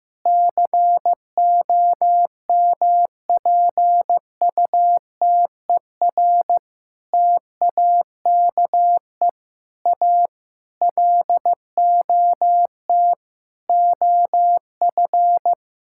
Morse code, heard as a keyed tone, about 16 wpm, 700 Hz. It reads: COMPUTER TAKE A LOT OF
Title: Sentences from Top 1000 Words - Review Part 6 - 15wpm